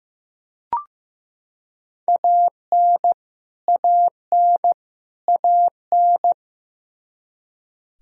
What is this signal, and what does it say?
Morse code, heard as a keyed tone, about 15 wpm, 700 Hz. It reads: AN AN AN
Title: Top 100 Words Repeated 15wpm